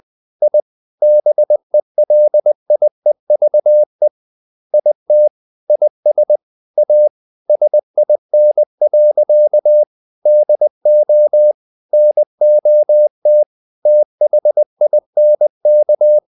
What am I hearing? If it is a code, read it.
Morse code, heard as a keyed tone, about 20 wpm, 600 Hz. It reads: I BELIEVE IT IS A SIN. DO NOT THINK
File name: the old man and the sea part 2 20wpm